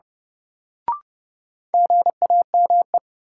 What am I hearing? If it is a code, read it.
Morse code, heard as a keyed tone, about 30 wpm, 700 Hz. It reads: GAME
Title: Sentences from Top 500 Words - New Word Drill - 30wpm